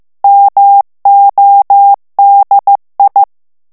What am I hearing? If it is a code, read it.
Morse code, heard as a keyed tone, about 15 wpm, 800 Hz. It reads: MODI
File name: ITALIEN texte clair 1 vitesse 14 mots minute